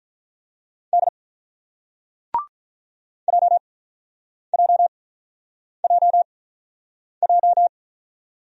Morse code, heard as a keyed tone, about 47 wpm, spaced as for 9 wpm, 700 Hz. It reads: D JJJJ